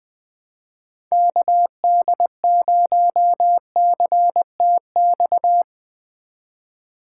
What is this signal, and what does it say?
Morse code, heard as a keyed tone, about 21 wpm, 700 Hz. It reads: KD0CTX
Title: Speed Racing - USA 2x3 Call Signs 20wpm